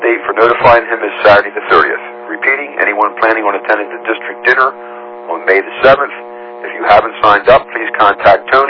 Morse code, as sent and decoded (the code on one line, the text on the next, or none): none